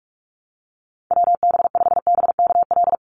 Morse code, heard as a keyed tone, about 44 wpm, 700 Hz. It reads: WB5BKL
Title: NA SOTA Chaser Callsigns 45wpm